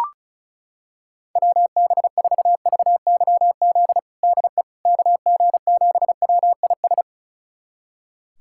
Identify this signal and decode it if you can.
Morse code, heard as a keyed tone, about 36 wpm, 690 Hz. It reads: WB4VYZ DE KG7WIS
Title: USA Callsign DE Callsign 35wpm